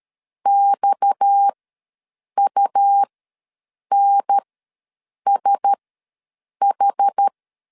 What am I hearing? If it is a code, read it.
Morse code, heard as a keyed tone, about 13 wpm, spaced as for 7 wpm, 800 Hz. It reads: XUNSH
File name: LEZIONI / 7 GF9Z / 7 G ws 12wpm 3140